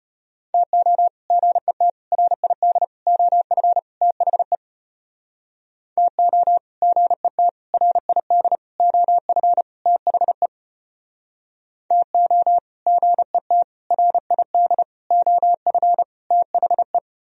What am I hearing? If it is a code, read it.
Morse code, heard as a keyed tone, about 34 wpm, 700 Hz. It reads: TO GET RID OF THE TO GET RID OF THE TO GET RID OF THE